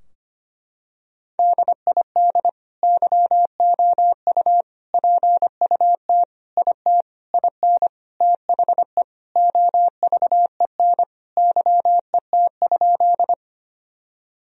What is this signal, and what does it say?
Morse code, heard as a keyed tone, about 25 wpm, 700 Hz. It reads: DID YOU PUT IT IN THE OVEN YET?
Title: Sentences from Top 1000 Words - Review Part 4 - 25wpm